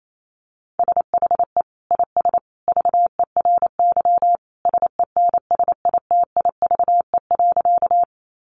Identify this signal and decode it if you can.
Morse code, heard as a keyed tone, about 28 wpm, 700 Hz. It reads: SHE IS VERY SENSITIVE.
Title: Sentences from Top 1400 Words 28wpm